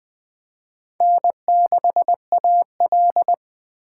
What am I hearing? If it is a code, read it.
Morse code, heard as a keyed tone, about 20 wpm, 700 Hz. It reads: N6AL
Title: USA Call Signs - 2x Character Spacing - 20wpm